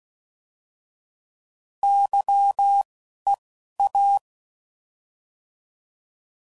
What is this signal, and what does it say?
Morse code, heard as a keyed tone, about 16 wpm, spaced as for 11 wpm, 800 Hz. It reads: YEA